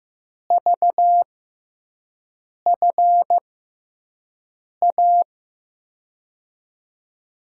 Morse code, heard as a keyed tone, about 15 wpm, spaced as for 5 wpm, 700 Hz. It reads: VFA
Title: USA Call Signs - 6 Characters - 6x Character Spacing - 15wpm